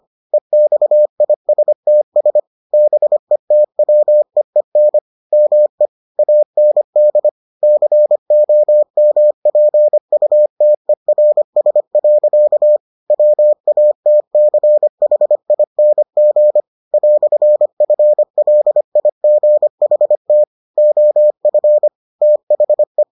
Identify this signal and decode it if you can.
Morse code, heard as a keyed tone, about 25 wpm, 600 Hz. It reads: EXISTS BETWEEN ME AND COMPUTERS. WATCHING "FLIGHT OF THE